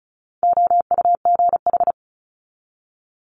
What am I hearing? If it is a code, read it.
Morse code, heard as a keyed tone, about 36 wpm, 700 Hz. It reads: OUGH
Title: Multi-letter Phonemes 35wpm